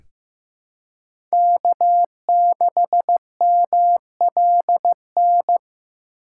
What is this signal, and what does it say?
Morse code, heard as a keyed tone, about 15 wpm, 700 Hz. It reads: K6MLN
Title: USA Call Signs - 5 Characters - 3x Character Spacing - 15wpm